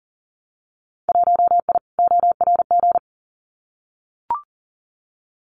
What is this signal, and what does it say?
Morse code, heard as a keyed tone, about 40 wpm, 700 Hz. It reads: 1I ORG
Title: Winter Field Day Exchanges 40wpm